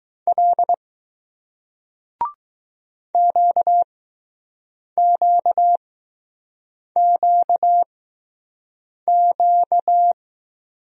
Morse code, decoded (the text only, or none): L QQQQ